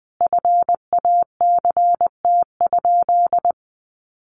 Morse code, heard as a keyed tone, about 20 wpm, 700 Hz. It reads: FACT?
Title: Sentences from Top 500 Words 20wpm